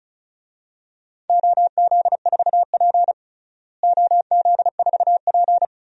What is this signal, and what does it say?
Morse code, heard as a keyed tone, about 36 wpm, 700 Hz. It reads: OZ4P OZ4P
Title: DXCC-100 Call Signs - 4 Characters - Repeated 2x - 35wpm